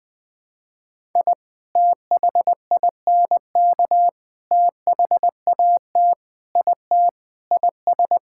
Morse code, heard as a keyed tone, about 20 wpm, 700 Hz. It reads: I THINK THAT IT IS
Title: Sentences with 3-Word Phrases - 20wpm